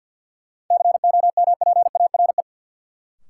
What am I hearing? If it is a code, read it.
Morse code, heard as a keyed tone, about 52 wpm, 700 Hz. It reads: COMPARE
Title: Sentences from Top 1300 Words - New Word Drill Alt - 50wpm